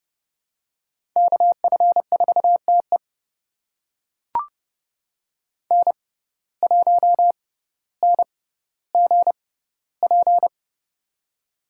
Morse code, heard as a keyed tone, about 30 wpm, spaced as for 15 wpm, 700 Hz. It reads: KF4TE N1NGP
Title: USA Call Signs - 5 Characters - 6x Character Spacing - 30wpm